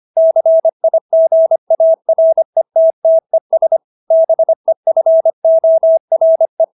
Morse code, heard as a keyed tone, about 25 wpm, 650 Hz. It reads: CIGARETTES BEFORE